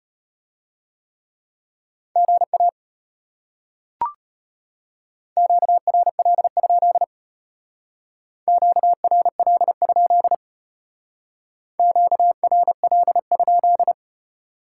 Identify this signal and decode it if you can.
Morse code, heard as a keyed tone, about 33 wpm, 700 Hz. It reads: GA QRL? QRL? QRL?